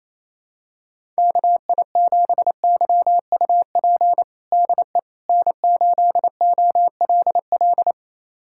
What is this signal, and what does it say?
Morse code, heard as a keyed tone, about 28 wpm, 700 Hz. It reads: KI7YUP DE N8OLL
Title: USA Callsign DE Callsign 28wpm